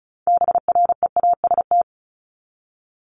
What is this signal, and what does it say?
Morse code, heard as a keyed tone, about 34 wpm, 700 Hz. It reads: BREAST
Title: Top 1751-2000 Words 35wpm